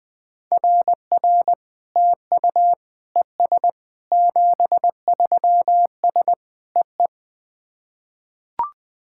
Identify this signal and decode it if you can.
Morse code, heard as a keyed tone, about 20 wpm, 700 Hz. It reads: RR TU ES 73S EE
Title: SOTA QSO Practice 20wpm